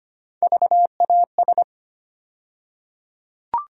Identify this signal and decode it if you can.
Morse code, heard as a keyed tone, about 25 wpm, 700 Hz. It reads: VAS
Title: USA Call Sign Postfixes - 3 Letters 25wpm